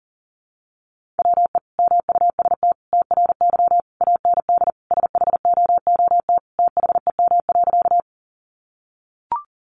Morse code, heard as a keyed tone, about 40 wpm, 700 Hz. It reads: WE MUST TRY AND SHOOT THEM.
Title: Sentences from Top 1200 Words 40wpm